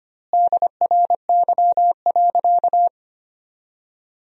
Morse code, heard as a keyed tone, about 25 wpm, 700 Hz. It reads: DRY.
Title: 2x Word-Spacing - Sentences from Top 500 Words - 25wpm